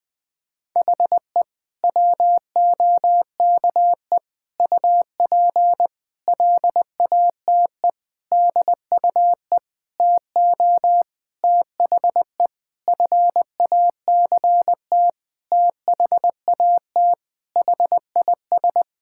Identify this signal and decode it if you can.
Morse code, heard as a keyed tone, about 21 wpm, 700 Hz. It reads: HE WOKE UP LATE DUE TO THE FACT THAT HIS
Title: Sentences with 4-Word Phrases - 20wpm